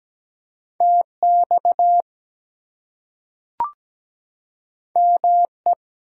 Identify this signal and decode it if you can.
Morse code, heard as a keyed tone, about 17 wpm, 700 Hz. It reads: TX ME
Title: US States and Canadian Provinces 17wpm